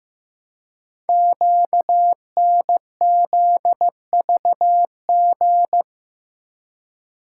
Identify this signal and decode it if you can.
Morse code, heard as a keyed tone, about 15 wpm, 700 Hz. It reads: QNZVG
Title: Code Groups - 5 Letters - 15wpm